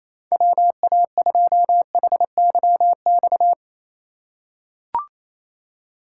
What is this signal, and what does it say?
Morse code, heard as a keyed tone, about 28 wpm, 700 Hz. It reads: WA2HYX
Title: USA Call Signs - 6 Characters - 4x Character Spacing - 28wpm